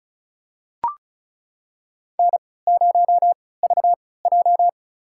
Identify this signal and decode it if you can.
Morse code, heard as a keyed tone, about 36 wpm, spaced as for 20 wpm, 700 Hz. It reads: N0VJ
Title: USA Call Signs - 3x Character Spacing - 35wpm